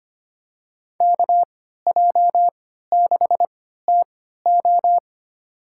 Morse code, heard as a keyed tone, about 25 wpm, spaced as for 14 wpm, 700 Hz. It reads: KJ6TO
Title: USA Call Signs - 5 Characters - 3x Character Spacing - 25wpm